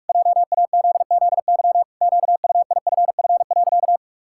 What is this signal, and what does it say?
Morse code, heard as a keyed tone, about 48 wpm, 700 Hz. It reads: JAZZY QUIFF.
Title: Pangram Challenge v2 - 45wpm